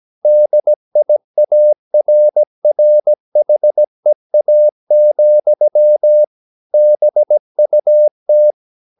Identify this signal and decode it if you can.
Morse code, heard as a keyed tone, about 17 wpm, 600 Hz. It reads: DIARRHEA, BUT
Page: Romanen Never af Ken Follet på engelsk.